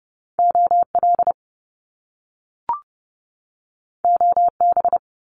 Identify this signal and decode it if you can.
Morse code, heard as a keyed tone, about 30 wpm, 700 Hz. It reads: OL OB
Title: Top Two Letter Combinations in Words 30wpm